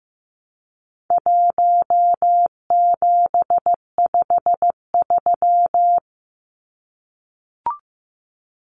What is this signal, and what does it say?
Morse code, heard as a keyed tone, about 15 wpm, 700 Hz. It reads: 1753
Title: Numbers - 4-Digit 15wpm